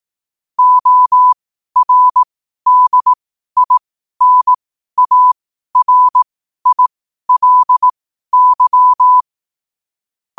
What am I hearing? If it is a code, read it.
Morse code, heard as a keyed tone, about 18 wpm, spaced as for 12 wpm, 1000 Hz. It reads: ORDINARILY
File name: 12wpm